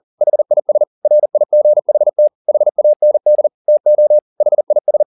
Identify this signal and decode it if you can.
Morse code, heard as a keyed tone, about 40 wpm, 600 Hz. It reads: HIS RIGHT HAND TO HIS
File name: the old man and the sea 2 40wpm